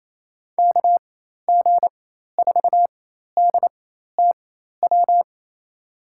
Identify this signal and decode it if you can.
Morse code, heard as a keyed tone, about 28 wpm, spaced as for 13 wpm, 700 Hz. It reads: KG4DTW
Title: USA Call Signs - 6 Characters - 4x Character Spacing - 28wpm